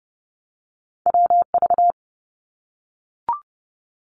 Morse code, heard as a keyed tone, about 30 wpm, 700 Hz. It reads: WV
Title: US States Repeated 3x 30wpm